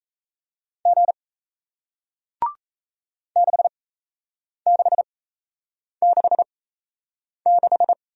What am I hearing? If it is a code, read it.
Morse code, heard as a keyed tone, about 37 wpm, spaced as for 8 wpm, 700 Hz. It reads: G 6666